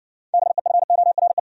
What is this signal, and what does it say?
Morse code, heard as a keyed tone, about 60 wpm, 700 Hz. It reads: BEFORE